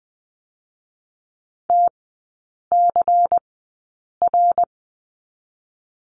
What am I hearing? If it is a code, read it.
Morse code, heard as a keyed tone, about 20 wpm, spaced as for 8 wpm, 700 Hz. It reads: TCR
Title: 121 Lesson 20e - Number 3 - Three Characters 20wpm